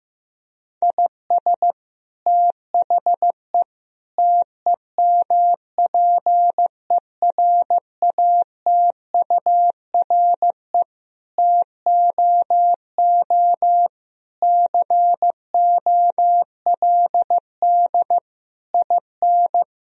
Morse code, heard as a keyed tone, about 15 wpm, 700 Hz. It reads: IS THE TEMPERATURE TOO COLD IN
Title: Sentences from Top 1000 Words - Review Part 4 - 15wpm